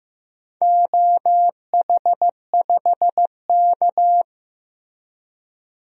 Morse code, heard as a keyed tone, about 15 wpm, 700 Hz. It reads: OH5K
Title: DXCC-100 Call Signs - 4 Characters - 15wpm